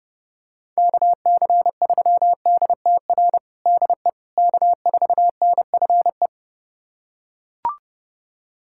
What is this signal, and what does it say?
Morse code, heard as a keyed tone, about 32 wpm, 700 Hz. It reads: KC3DTR DE K4NFE
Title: USA Callsign DE Callsign 30wpm